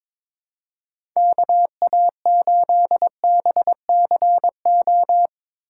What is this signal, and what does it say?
Morse code, heard as a keyed tone, about 21 wpm, 700 Hz. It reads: KA8BCO
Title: USA Call Signs - 6 Characters - 2x Character Spacing - 22wpm